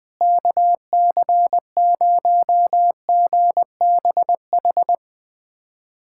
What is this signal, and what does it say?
Morse code, heard as a keyed tone, about 20 wpm, 690 Hz. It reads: KC0GBH
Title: USA Callsign DE Callsign 20wpm